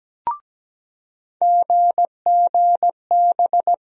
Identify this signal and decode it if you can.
Morse code, heard as a keyed tone, about 17 wpm, 700 Hz. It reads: GGB
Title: USA Call Sign Postfixes - 3 Letters 17wpm